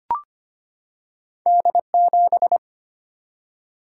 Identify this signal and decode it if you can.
Morse code, heard as a keyed tone, about 25 wpm, 700 Hz. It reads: D7
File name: Two Letter-Number Combinations - 25wpm